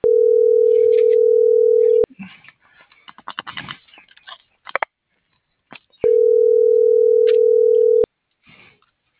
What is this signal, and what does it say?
Ringback tone (North America), 440 + 480 Hz.